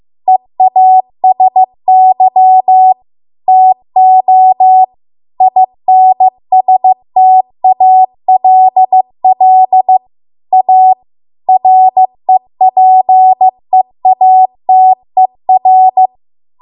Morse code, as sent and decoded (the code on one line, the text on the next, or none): . .- ... -.-- / - --- / .. -. ... - .- .-.. .-.. / .- / .-. .--. .- - . .-.
EASY TO INSTALL A REPEATER